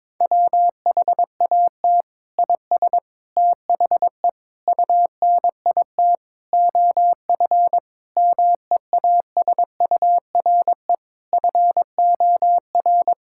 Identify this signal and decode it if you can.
Morse code, heard as a keyed tone, about 22 wpm, 700 Hz. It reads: WHAT IS THE UNIT OF MEASURE FOR
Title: Sentences from Top 1000 Words - Review Part 6 - 22wpm